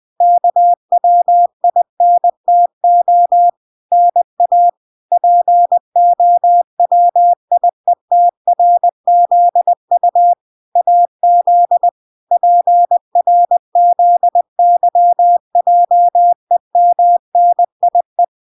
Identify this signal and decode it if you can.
Morse code, heard as a keyed tone, about 20 wpm, 700 Hz. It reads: KWINTO NA POWIETRZU AZ PRZYJEMNIE